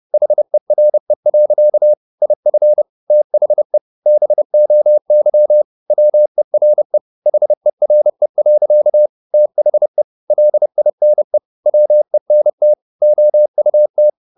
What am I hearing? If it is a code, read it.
Morse code, heard as a keyed tone, about 30 wpm, 600 Hz. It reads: HERE. IF THE BOY WERE HERE. THE LINE WENT OUT